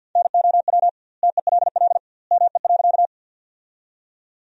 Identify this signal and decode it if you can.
Morse code, heard as a keyed tone, about 50 wpm, 700 Hz. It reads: NOW TELL ME.
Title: 2x Word-Spacing - Sentences from Top 200 Words - 50wpm